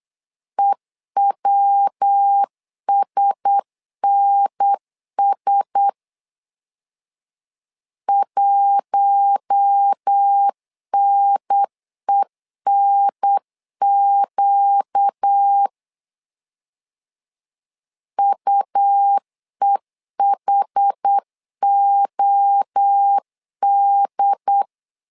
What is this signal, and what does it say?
Morse code, heard as a keyed tone, about 8 wpm, 800 Hz. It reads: EWSNS 1NENQ UEHOD
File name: LEZIONI / 4 Q1S 8wpm 2710